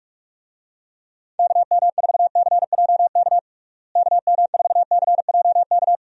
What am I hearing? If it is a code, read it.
Morse code, heard as a keyed tone, about 46 wpm, 700 Hz. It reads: KM4CJK KM4CJK
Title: USA Call Signs - 6 Characters - Repeated 3x - 45wpm